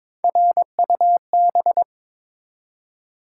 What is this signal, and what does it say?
Morse code, heard as a keyed tone, about 23 wpm, 700 Hz. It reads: RUB
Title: Three Letter Words - 22wpm